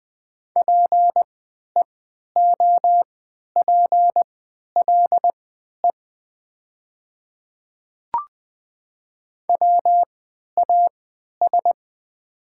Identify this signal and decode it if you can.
Morse code, heard as a keyed tone, about 20 wpm, spaced as for 11 wpm, 700 Hz. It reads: PEOPLE WAS
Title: 3x Character Spacing - Top 200 Words - 20wpm